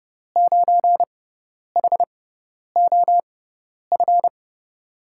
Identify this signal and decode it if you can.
Morse code, heard as a keyed tone, about 30 wpm, spaced as for 10 wpm, 700 Hz. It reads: 9HOF
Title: USA Call Signs - 5 Characters - 6x Character Spacing - 30wpm